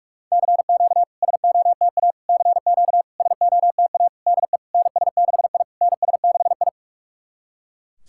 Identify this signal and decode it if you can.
Morse code, heard as a keyed tone, about 46 wpm, 690 Hz. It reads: CQ SOTA CQ SOTA DE NS6I NS6I
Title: SOTA QSO Practice 45wpm